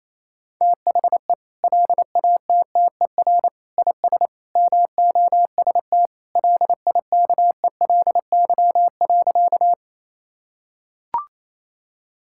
Morse code, heard as a keyed tone, about 28 wpm, 700 Hz. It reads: THE LATTER IS MOST LIKELY.
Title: Sentences from Top 1600 Words 28wpm